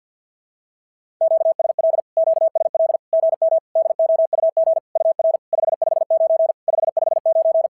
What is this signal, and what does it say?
Morse code, heard as a keyed tone, about 50 wpm, 650 Hz. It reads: QSL QSL GM DOUG UR 559 559
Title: SOTA QSO Practice 50wpm